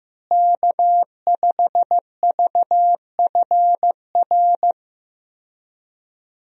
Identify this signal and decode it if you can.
Morse code, heard as a keyed tone, about 16 wpm, 700 Hz. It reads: K5VFR
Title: USA Call Signs - 5 Characters - Repeated 2x - 15wpm